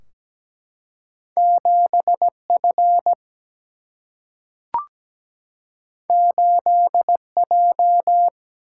Morse code, heard as a keyed tone, about 17 wpm, 700 Hz. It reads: 7 F 8 J
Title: Single-Letter Single-Number 17wpm